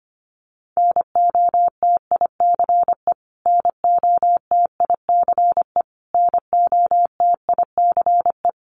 Morse code, heard as a keyed tone, about 25 wpm, 700 Hz. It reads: NOTICE NOTICE NOTICE
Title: Top 500 Words Repeated 25wpm